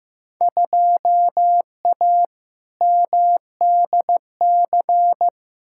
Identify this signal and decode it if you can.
Morse code, heard as a keyed tone, about 15 wpm, 700 Hz. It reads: 2A MDC
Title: Field Day Exchanges - 15wpm